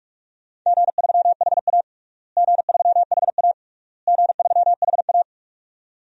Morse code, heard as a keyed tone, about 42 wpm, 700 Hz. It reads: G3HA G3HA G3HA
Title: DXCC-100 Call Signs - 4 Characters - Repeated 3x - 45wpm